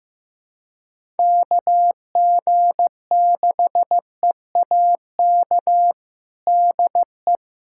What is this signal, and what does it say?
Morse code, heard as a keyed tone, about 15 wpm, 700 Hz. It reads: KG6EAK DE
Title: USA Callsign DE Callsign - 6 Characters - 15wpm